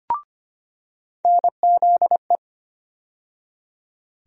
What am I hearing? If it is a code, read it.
Morse code, heard as a keyed tone, about 25 wpm, 700 Hz. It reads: NZE
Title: USA Call Sign Postfixes - 3 Letters 25wpm